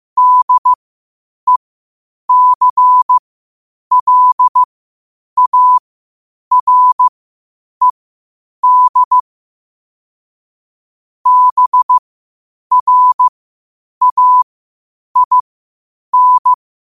Morse code, heard as a keyed tone, about 15 wpm, spaced as for 8 wpm, 1000 Hz. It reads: DECLARED BRAIN
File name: qotdfiles / 8wpm